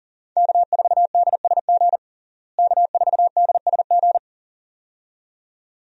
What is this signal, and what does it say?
Morse code, heard as a keyed tone, about 40 wpm, 700 Hz. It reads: K4DSG K4DSG
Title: USA Call Signs - 5 Characters - Repeated 2x - 40wpm